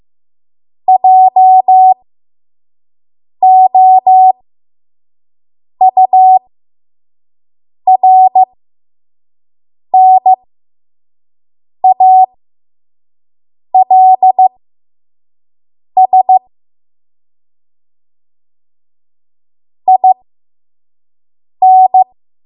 Morse code, heard as a keyed tone, about 15 wpm, spaced as for 5 wpm, 750 Hz. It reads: JOURNALS IN
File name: Morse / 05WPM